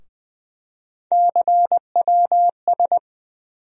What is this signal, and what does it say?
Morse code, heard as a keyed tone, about 20 wpm, 700 Hz. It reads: CWS
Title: USA Call Sign Postfixes - 3 Letters 20wpm